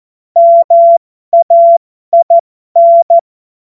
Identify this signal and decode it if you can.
Morse code, heard as a keyed tone, about 14 wpm, spaced as for 12 wpm, 680 Hz. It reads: MAIN
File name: Morsekurs / Witze